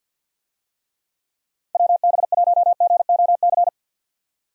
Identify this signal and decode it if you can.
Morse code, heard as a keyed tone, about 52 wpm, 700 Hz. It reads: WB1GOC
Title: USA Call Signs - 6 Characters - Repeated 3x - 50wpm